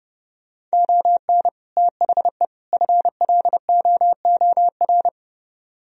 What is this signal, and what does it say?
Morse code, heard as a keyed tone, about 40 wpm, 700 Hz. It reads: ON THE FLOOR